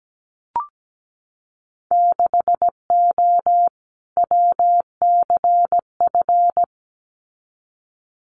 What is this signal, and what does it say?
Morse code, heard as a keyed tone, about 17 wpm, 700 Hz. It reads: 6O WCF
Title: Winter Field Day Exchanges Encore 17wpm